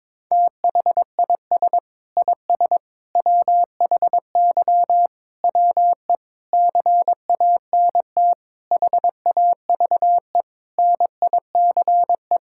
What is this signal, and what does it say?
Morse code, heard as a keyed tone, about 22 wpm, 700 Hz. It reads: THIS IS WHY WE CANT HAVE NICE
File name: practice / 22wpm / Sets of 8 Words - 22wpm